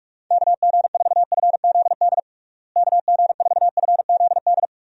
Morse code, heard as a keyed tone, about 41 wpm, 700 Hz. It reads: KG4FZD KG4FZD
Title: USA Call Signs - 6 Characters - Repeated 2x - 45wpm